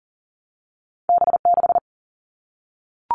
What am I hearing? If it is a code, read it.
Morse code, heard as a keyed tone, about 41 wpm, 700 Hz. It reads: B6
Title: All Cut Numbers - 2-Digit 40wpm